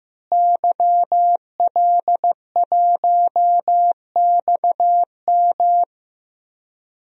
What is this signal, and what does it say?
Morse code, heard as a keyed tone, about 16 wpm, 700 Hz. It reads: YL1XM
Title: DXCC-100 Call Signs - 5 Characters - Repeated 2x - 15wpm